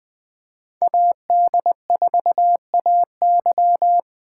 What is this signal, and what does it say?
Morse code, heard as a keyed tone, about 21 wpm, 700 Hz. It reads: AD4AY